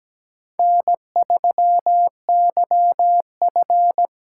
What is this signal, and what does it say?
Morse code, heard as a keyed tone, about 17 wpm, 700 Hz. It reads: N3YF
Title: USA Call Signs - 8x Character Spacing - 17wpm